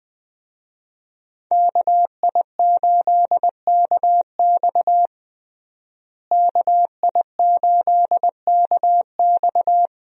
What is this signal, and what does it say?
Morse code, heard as a keyed tone, about 20 wpm, 700 Hz. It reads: KI8KX KI8KX
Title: USA Call Signs - 5 Characters - Repeated 3x - 20wpm